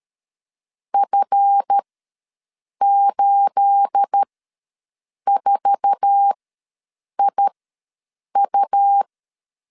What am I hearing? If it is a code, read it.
Morse code, heard as a keyed tone, about 13 wpm, spaced as for 7 wpm, 800 Hz. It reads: F84IU
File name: LEZIONI / 10 SIGNS / 10C DOT ws 12wpm 3548